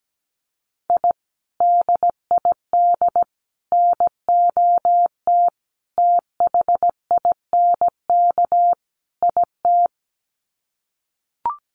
Morse code, heard as a keyed tone, about 17 wpm, 700 Hz. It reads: I DID NOT THINK IT
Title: Sets of 5 Words Encore2 17wpm